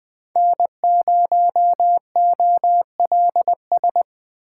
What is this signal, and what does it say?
Morse code, heard as a keyed tone, about 20 wpm, 700 Hz. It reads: N0OLS
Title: USA Call Signs - 5 Characters - 2x Character Spacing - 20wpm